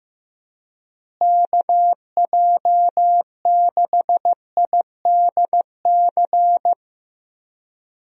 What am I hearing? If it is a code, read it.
Morse code, heard as a keyed tone, about 15 wpm, 700 Hz. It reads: KJ6IDC
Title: USA Call Signs - 6 Characters - Repeated 2x - 15wpm